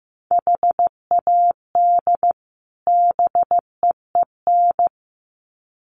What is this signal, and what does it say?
Morse code, heard as a keyed tone, about 15 wpm, 700 Hz. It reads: HAD BEEN
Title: Sets of 2 Words 15wpm